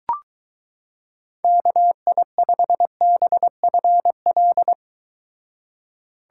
Morse code, heard as a keyed tone, about 22 wpm, 700 Hz. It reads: KI5BFL